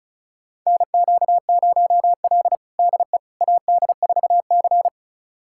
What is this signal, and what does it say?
Morse code, heard as a keyed tone, about 34 wpm, 700 Hz. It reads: NQ0L DE AD4C
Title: USA Callsign DE Callsign - 4 Characters - 35wpm